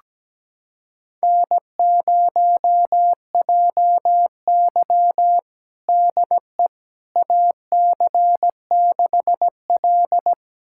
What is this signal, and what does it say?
Morse code, heard as a keyed tone, about 17 wpm, 700 Hz. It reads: N0JY DE AC6L
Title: USA Callsign DE Callsign - 4 Characters - 17wpm